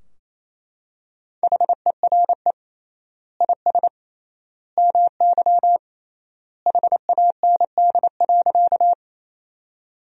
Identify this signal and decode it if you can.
Morse code, heard as a keyed tone, about 28 wpm, 700 Hz. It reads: HERE IS MY HAND.
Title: 3x Word-Spacing - Sentences from Top 200 Words - 28wpm